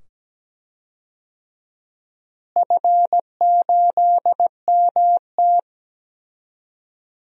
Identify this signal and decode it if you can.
Morse code, heard as a keyed tone, about 17 wpm, 700 Hz. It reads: F8MT
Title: DXCC-100 Call Signs - 4 Characters - Repeated 2x - 17wpm